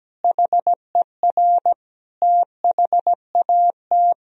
Morse code, heard as a keyed tone, about 17 wpm, 700 Hz. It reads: HER THAT
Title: Sentences from Top 1000 Words - Review Part 2 - 17wpm